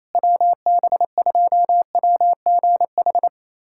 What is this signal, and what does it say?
Morse code, heard as a keyed tone, about 28 wpm, 700 Hz. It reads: WB2WGH
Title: USA Callsign DE Callsign - 6 Characters - 28wpm